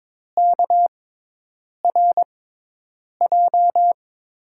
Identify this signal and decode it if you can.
Morse code, heard as a keyed tone, about 22 wpm, spaced as for 8 wpm, 700 Hz. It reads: KRJ
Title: USA Call Signs - 6 Characters - 6x Character Spacing - 22wpm